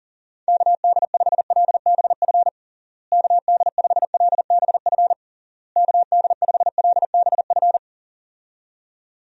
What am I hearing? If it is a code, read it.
Morse code, heard as a keyed tone, about 37 wpm, 700 Hz. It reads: KD5LBF KD5LBF KD5LBF
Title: USA Call Signs - 6 Characters - Repeated 3x - 40wpm